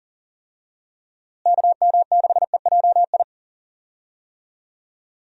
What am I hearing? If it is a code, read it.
Morse code, heard as a keyed tone, about 40 wpm, 700 Hz. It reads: KM6EJI
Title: USA Call Signs - 6 Characters - Repeated 3x - 40wpm